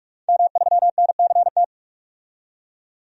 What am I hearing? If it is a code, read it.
Morse code, heard as a keyed tone, about 45 wpm, 700 Hz. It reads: M3NCT
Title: DXCC-100 Call Signs - 5 Characters - 45wpm